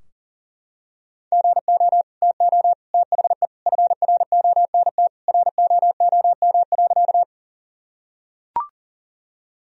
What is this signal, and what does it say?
Morse code, heard as a keyed tone, about 40 wpm, 700 Hz. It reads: GO TO THE FRONT ROOM.
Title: Sentences from Top 1000 Words - Review Part 2 - 40wpm